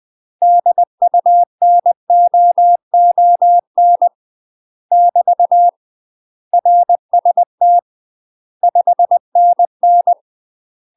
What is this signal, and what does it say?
Morse code, heard as a keyed tone, about 20 wpm, 700 Hz. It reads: DUNOON <BT> RST 5NN
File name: UK MASTER QSOs TRAINING / 20WPM / UK MASTER QSOs 0044